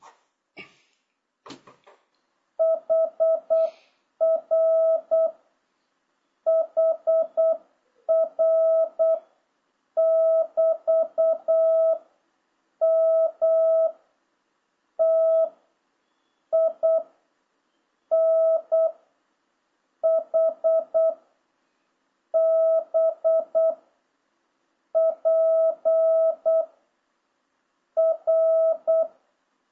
Morse code, heard as a keyed tone, about 8 wpm, 650 Hz. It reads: HR HR <BT> M T I N H B P R